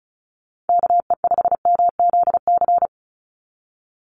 Morse code, heard as a keyed tone, about 34 wpm, 700 Hz. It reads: KE5MZC
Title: USA Call Signs 35wpm